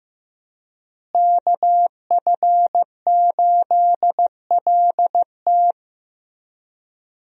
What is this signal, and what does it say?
Morse code, heard as a keyed tone, about 16 wpm, 700 Hz. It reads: KF8LT
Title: USA Call Signs - 5 Characters - Repeated 3x - 15wpm